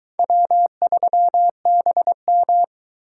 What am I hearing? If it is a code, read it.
Morse code, heard as a keyed tone, about 22 wpm, 700 Hz. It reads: W3BM